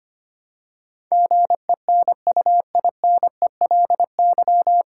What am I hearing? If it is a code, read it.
Morse code, heard as a keyed tone, about 25 wpm, 700 Hz. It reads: GENUINELY
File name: practice / 25wpm / 9 Letter Words - First 3 Letters Given - 25wpm